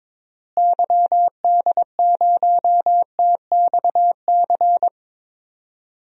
Morse code, heard as a keyed tone, about 21 wpm, 700 Hz. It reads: YD0TXC
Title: DXCC-100 Call Signs - 6 Characters - Repeated 3x - 22wpm